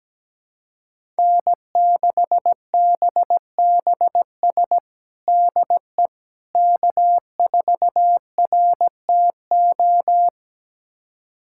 Morse code, heard as a keyed tone, about 17 wpm, 700 Hz. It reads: N6BBS DE K4RTO
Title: USA Callsign DE Callsign - 5 Characters - 17wpm